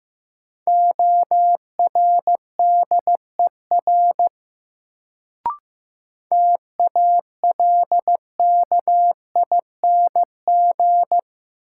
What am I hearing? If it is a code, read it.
Morse code, heard as a keyed tone, about 15 wpm, 700 Hz. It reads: ORDER TALKING
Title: Sets of 2 Words - New Word Drill - 15wpm